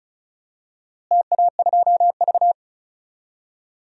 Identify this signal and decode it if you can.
Morse code, heard as a keyed tone, about 34 wpm, 700 Hz. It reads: TA2V
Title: DXCC-100 Call Signs - 4 Characters - Repeated 3x - 35wpm